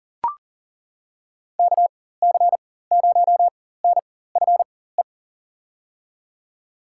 Morse code, heard as a keyed tone, about 40 wpm, spaced as for 19 wpm, 700 Hz. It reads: KC0NFE
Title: USA Call Signs - 6 Characters - 4x Character Spacing - 40wpm